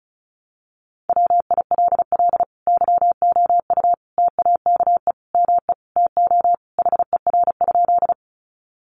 Morse code, heard as a keyed tone, about 34 wpm, 700 Hz. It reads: WILL YOU TAKE ME TO HER?
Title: Sentences from Top 200 Words 35wpm